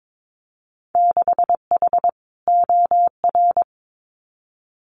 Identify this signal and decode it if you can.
Morse code, heard as a keyed tone, about 22 wpm, 700 Hz. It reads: 6H OR
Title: Winter Field Day Exchanges Encore 22wpm